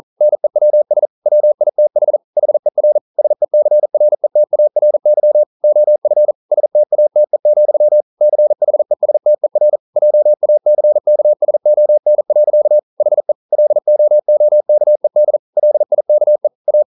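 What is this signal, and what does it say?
Morse code, heard as a keyed tone, about 41 wpm, 600 Hz. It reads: NEWS WITH HER SECRETARY OF STATE, CHESTER JACKSON. HE LOOKED LIKE A
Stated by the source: Morse code practice at 41 wpm, from CW med Gnister.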